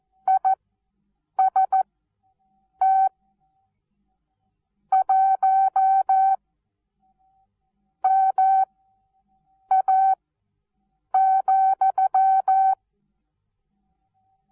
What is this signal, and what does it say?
Morse code, heard as a keyed tone, about 14 wpm, spaced as for 6 wpm, 770 Hz. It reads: IST 1MA,
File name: Onlinelehrgang / morsen / DJ4UF Morsen19